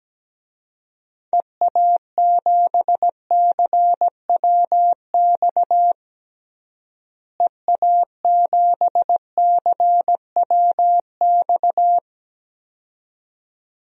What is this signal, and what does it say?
Morse code, heard as a keyed tone, about 17 wpm, 700 Hz. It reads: EA7CWX EA7CWX
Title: DXCC-100 Call Signs - 6 Characters - Repeated 2x - 17wpm